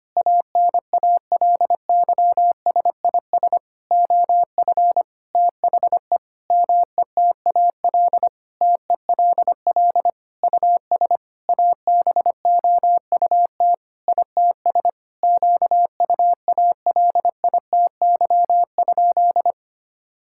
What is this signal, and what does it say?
Morse code, heard as a keyed tone, about 25 wpm, 700 Hz. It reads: ANALYSIS OF THE METAL TELL US ABOUT ITS QUALITY?
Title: Sentences from Top 1000 Words - Review Part 6 - 25wpm